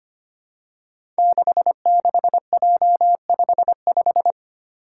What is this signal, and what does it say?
Morse code, heard as a keyed tone, about 25 wpm, 700 Hz. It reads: 66J55
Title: Five Challenging Letter-Number Combinations 25wpm